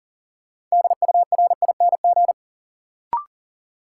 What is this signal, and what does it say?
Morse code, heard as a keyed tone, about 39 wpm, 700 Hz. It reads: DURING
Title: Sets of 2 Words - New Word Drill - 40wpm